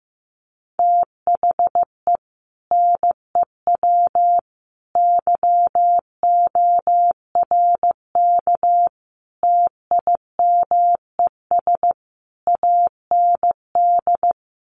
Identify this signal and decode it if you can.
Morse code, heard as a keyed tone, about 15 wpm, 700 Hz. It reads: THE NEW YORK TIMES AND
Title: Sets of 5 Words Encore 15wpm